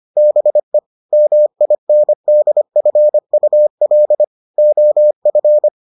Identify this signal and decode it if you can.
Morse code, heard as a keyed tone, about 25 wpm, 600 Hz. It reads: BE MINDFUL OF